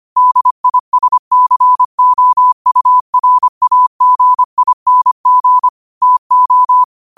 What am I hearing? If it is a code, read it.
Morse code, heard as a keyed tone, about 25 wpm, 1000 Hz. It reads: DISCOURAGING TO
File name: qotdfiles / 25wpm